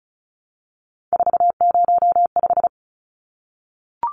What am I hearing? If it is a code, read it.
Morse code, heard as a keyed tone, about 34 wpm, 700 Hz. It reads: 405
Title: Numbers - 3-Digit 35wpm